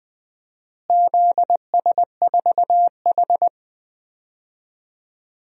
Morse code, heard as a keyed tone, about 20 wpm, 700 Hz. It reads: ZS4H
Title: DXCC-100 Call Signs - 4 Characters - Repeated 3x - 20wpm